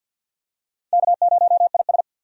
Morse code, heard as a keyed tone, about 52 wpm, 700 Hz. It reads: K0IS
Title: USA Call Signs - 4 Characters - 50wpm